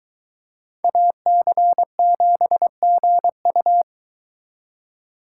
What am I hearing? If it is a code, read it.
Morse code, heard as a keyed tone, about 22 wpm, 700 Hz. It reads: AC7GU